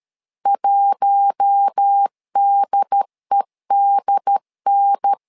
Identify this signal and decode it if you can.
Morse code, heard as a keyed tone, about 13 wpm, 800 Hz. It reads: 1DEDN
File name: LEZIONI / 4 Q1S 12wpm 1802